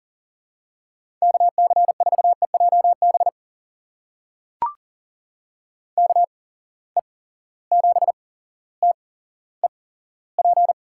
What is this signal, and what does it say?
Morse code, heard as a keyed tone, about 40 wpm, spaced as for 17 wpm, 700 Hz. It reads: KC4EJB KE7TEP
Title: USA Call Signs - 8x Character Spacing - 40wpm